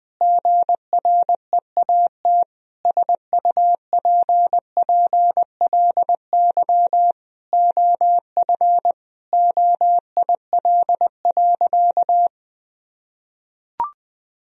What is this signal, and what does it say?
Morse code, heard as a keyed tone, about 20 wpm, 700 Hz. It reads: GREAT SUPPLY OF OIL.
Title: Sentences from Top 1000 Words - Review Part 4 - 20wpm